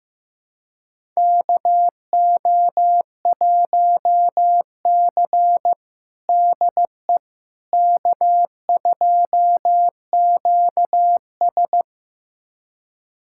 Morse code, heard as a keyed tone, about 15 wpm, 700 Hz. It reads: KO1C DE K2QS
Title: USA Callsign DE Callsign - 4 Characters - 15wpm